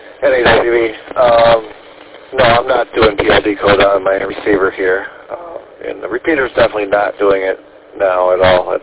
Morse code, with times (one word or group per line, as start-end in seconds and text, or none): none